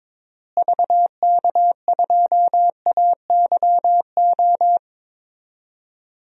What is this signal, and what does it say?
Morse code, heard as a keyed tone, about 22 wpm, 700 Hz. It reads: VK2AYO
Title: DXCC-100 Call Signs - 6 Characters - Repeated 3x - 22wpm